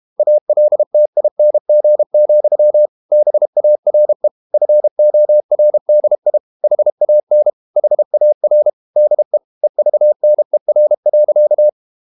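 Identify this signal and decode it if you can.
Morse code, heard as a keyed tone, about 32 wpm, 600 Hz. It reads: ALTING, BARE FORDI HAN HAR DE EVNER.
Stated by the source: High-speed QRQ CW at 32 WPM.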